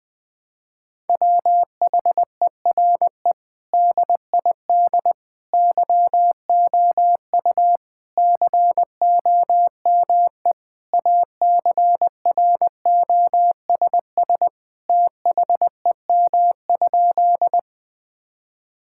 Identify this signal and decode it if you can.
Morse code, heard as a keyed tone, about 20 wpm, 700 Hz. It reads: WHERE DID YOU COME ACROSS THEM?
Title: Sentences from Top 1000 Words - Review Part 5 - 20wpm